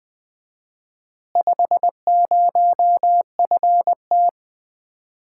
Morse code, heard as a keyed tone, about 20 wpm, 700 Hz. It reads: 50FT
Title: Common QSO Phrases - 4x Word-Spacing - 20wpm